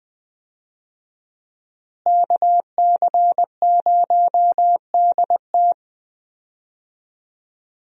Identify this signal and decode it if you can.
Morse code, heard as a keyed tone, about 20 wpm, 700 Hz. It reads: KC0DT
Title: USA Call Signs - 5 Characters - Repeated 3x - 20wpm